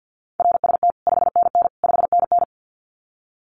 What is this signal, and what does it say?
Morse code, heard as a keyed tone, about 50 wpm, 700 Hz. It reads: RST 5NN 5NN
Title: Common QSO Phrases Encore 50wpm